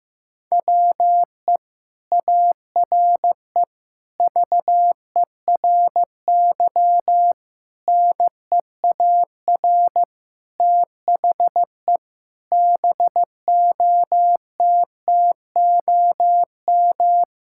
Morse code, heard as a keyed tone, about 15 wpm, 700 Hz. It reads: WE ARE VERY NEAR THE BOTTOM
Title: Sentences from Top 1000 Words - Review Part 6 - 15wpm